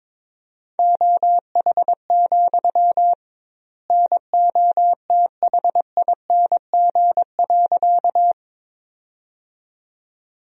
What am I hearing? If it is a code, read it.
Morse code, heard as a keyed tone, about 22 wpm, 700 Hz. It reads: OH, NOTHING.
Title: 2x Word-Spacing - Sentences from Top 500 Words - 22wpm